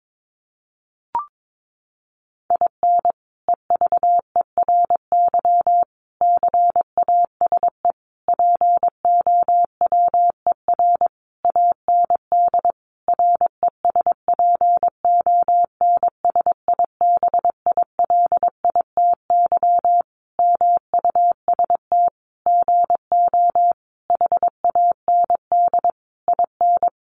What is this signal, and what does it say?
Morse code, heard as a keyed tone, about 22 wpm, 700 Hz. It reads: IN EVERY CASE POWER AND RESPONSIBILITY MUST GO HAND IN
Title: Sentences from Top 1200 Words 22wpm